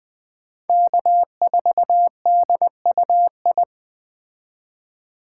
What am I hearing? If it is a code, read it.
Morse code, heard as a keyed tone, about 21 wpm, 700 Hz. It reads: K4DUI
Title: Speed Racing - USA Call Signs 17wpm